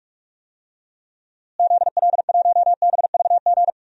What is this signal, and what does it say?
Morse code, heard as a keyed tone, about 41 wpm, 700 Hz. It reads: ZL1BVG
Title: DXCC-100 Call Signs - 6 Characters - Repeated 3x - 45wpm